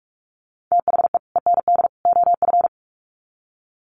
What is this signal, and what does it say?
Morse code, heard as a keyed tone, about 44 wpm, 700 Hz. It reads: THE END OF
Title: Sets of 3 Words 45wpm